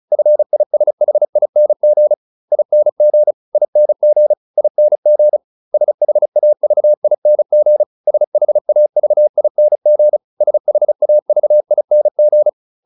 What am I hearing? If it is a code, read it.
Morse code, heard as a keyed tone, about 34 wpm, 600 Hz. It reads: FISHING ING ING ING SHAVING SHAVING SHAVING
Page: advanced-practice-files